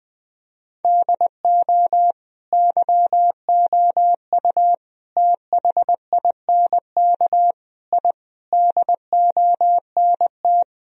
Morse code, heard as a keyed tone, about 20 wpm, 700 Hz. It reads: DO YOU THINK I DONT
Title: Sentences from Top 1000 Words - Review Part 1 - 20wpm